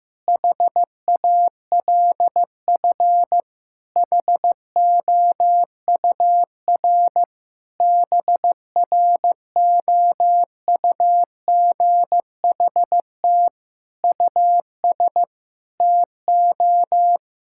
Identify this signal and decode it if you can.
Morse code, heard as a keyed tone, about 15 wpm, 700 Hz. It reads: HALF HOUR BROUGHT US TO
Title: Sentences from Top 1600 Words 15wpm